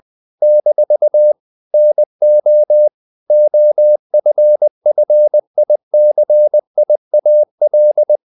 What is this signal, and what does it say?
Morse code, heard as a keyed tone, about 20 wpm, 600 Hz. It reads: - NO OFFICIAL
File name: CW / Practice / NationalCapitals20